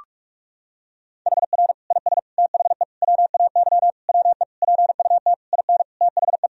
Morse code, heard as a keyed tone, about 46 wpm, 700 Hz. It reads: HR IS THE WAY WE PUT IN THE
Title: Sentences from Top 200 Words and Abbreviations 45wpm